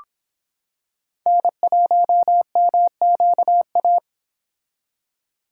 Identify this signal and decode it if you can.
Morse code, heard as a keyed tone, about 25 wpm, 700 Hz. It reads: N1MQA